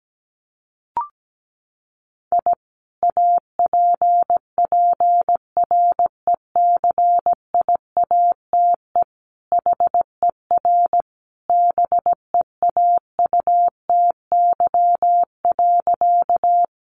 Morse code, heard as a keyed tone, about 17 wpm, 700 Hz. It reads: I APPRECIATE HER BEAUTY.
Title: Sentences from Top 1300 Words 17wpm